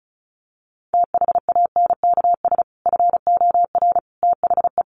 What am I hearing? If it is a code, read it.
Morse code, heard as a keyed tone, about 34 wpm, 700 Hz. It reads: THANKS FOR THE
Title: Sentences from Top 1500 Words 35wpm